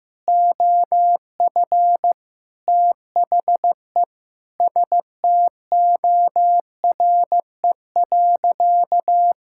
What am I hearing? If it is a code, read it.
Morse code, heard as a keyed tone, about 16 wpm, 700 Hz. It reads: OF THE STORE.
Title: Sentences from Top 1000 Words - Review Part 5 - 15wpm